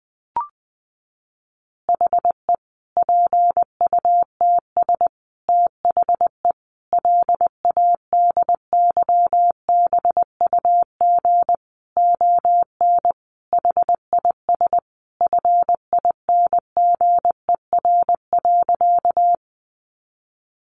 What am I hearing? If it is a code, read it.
Morse code, heard as a keyed tone, about 20 wpm, 700 Hz. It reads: HE PUTS THE LADYBUG ON HIS FINGER.
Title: Additional Easy Phrases 20wpm